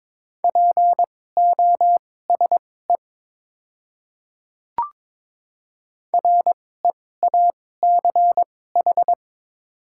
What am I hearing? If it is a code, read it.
Morse code, heard as a keyed tone, about 22 wpm, spaced as for 16 wpm, 690 Hz. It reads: POSE REACH
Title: 2x Character Spacing - Top 500 Words - 22wpm